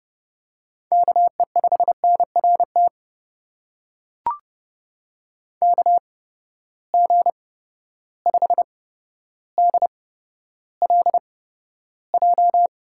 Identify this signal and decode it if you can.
Morse code, heard as a keyed tone, about 29 wpm, spaced as for 13 wpm, 700 Hz. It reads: KE5NRT KG5DLJ
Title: USA Call Signs - 6 Characters - 8x Character Spacing - 30wpm